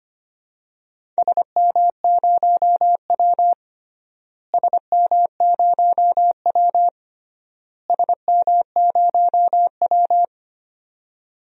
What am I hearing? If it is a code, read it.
Morse code, heard as a keyed tone, about 25 wpm, 700 Hz. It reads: SM0W SM0W SM0W
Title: DXCC-100 Call Signs - 4 Characters - Repeated 3x - 25wpm